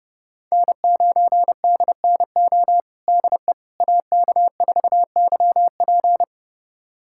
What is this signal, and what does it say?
Morse code, heard as a keyed tone, about 30 wpm, 700 Hz. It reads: N9DNO DE AK4YP
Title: USA Callsign DE Callsign - 5 Characters - 30wpm